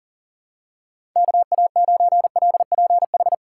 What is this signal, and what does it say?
Morse code, heard as a keyed tone, about 40 wpm, 700 Hz. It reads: KA9LPH
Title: USA Call Signs - 6 Characters - 4x Character Spacing - 40wpm